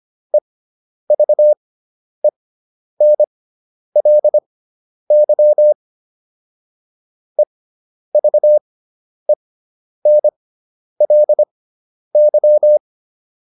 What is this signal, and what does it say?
Morse code, heard as a keyed tone, about 25 wpm, spaced as for 10 wpm, 600 Hz. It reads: EVENLY EVENLY